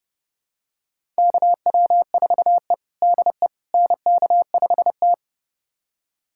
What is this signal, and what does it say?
Morse code, heard as a keyed tone, about 30 wpm, 700 Hz. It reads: KW4E DE NK5T
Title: USA Callsign DE Callsign - 4 Characters - 30wpm